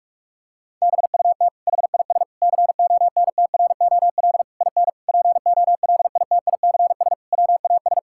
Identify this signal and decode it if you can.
Morse code, heard as a keyed tone, about 46 wpm, 700 Hz. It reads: BUT HIS CONTROL IN POLITICS WAS
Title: Sentences from Top 1000 Words - Review Part 6 - 45wpm